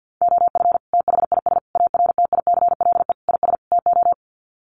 Morse code, heard as a keyed tone, about 50 wpm, 700 Hz. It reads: OF THIS ARTICLE IS TO
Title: Sets of 5 Words Encore2 50wpm